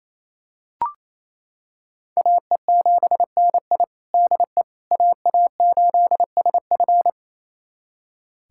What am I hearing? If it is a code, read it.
Morse code, heard as a keyed tone, about 28 wpm, 700 Hz. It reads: AE7NI DE AA8SF
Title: USA Callsign DE Callsign - 5 Characters - 28wpm